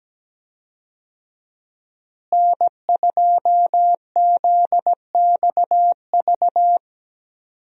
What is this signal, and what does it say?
Morse code, heard as a keyed tone, about 17 wpm, 700 Hz. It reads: N2ZXV
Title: USA Call Signs - 5 Characters - Repeated 3x - 17wpm